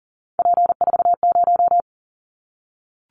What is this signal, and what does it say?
Morse code, heard as a keyed tone, about 40 wpm, 700 Hz. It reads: P40
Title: Callsign Prefixes for Top 100 DXCC - 40wpm